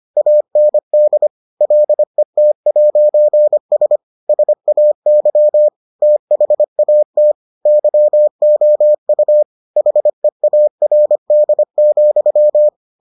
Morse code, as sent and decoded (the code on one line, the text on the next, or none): .- -. -.. / .-.. - .----. ... / ... .- -.-- / - .... .- - / -.-- --- ..- / .... .- .-. -.. --..--
AND LET'S SAY THAT YOU HEARD,